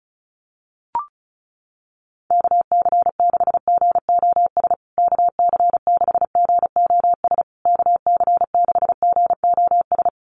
Morse code, heard as a keyed tone, about 34 wpm, 700 Hz. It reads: KC6GOS KC6GOS KC6GOS
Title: USA Call Signs Repeated 3x 35wpm